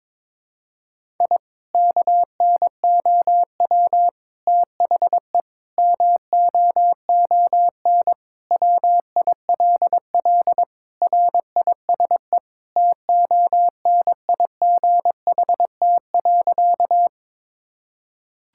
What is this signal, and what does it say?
Morse code, heard as a keyed tone, about 22 wpm, 700 Hz. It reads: I KNOW THE MOON WILL RISE TONIGHT.
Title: Sentences from Top 1000 Words - Review Part 6 - 22wpm